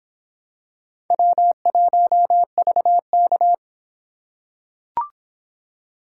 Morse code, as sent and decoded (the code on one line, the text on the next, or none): .-- .---- ...- -.-
W1VK